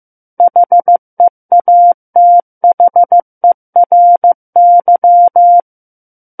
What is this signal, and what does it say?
Morse code, heard as a keyed tone, about 15 wpm, 700 Hz. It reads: HEATHERY
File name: Book One / 06-Chapter 6 - 15 WPM